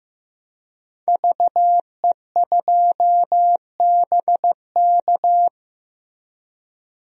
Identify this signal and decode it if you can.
Morse code, heard as a keyed tone, about 15 wpm, 700 Hz. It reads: VE2BK
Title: DXCC-100 Call Signs - 5 Characters - Repeated 3x - 15wpm